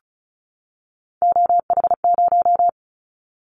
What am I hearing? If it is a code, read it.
Morse code, heard as a keyed tone, about 34 wpm, 700 Hz. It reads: OH0
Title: Callsign Prefixes for Top 100 DXCC - 35wpm